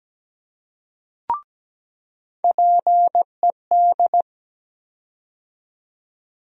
Morse code, heard as a keyed tone, about 17 wpm, 700 Hz. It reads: PED
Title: USA Call Sign Postfixes - 3 Letters 17wpm